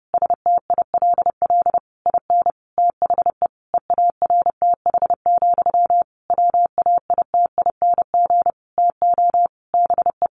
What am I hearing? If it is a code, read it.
Morse code, heard as a keyed tone, about 28 wpm, 700 Hz. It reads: STILL IN THE EARTH, WAITING TO BE
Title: A Sound Of Thunder 30wpm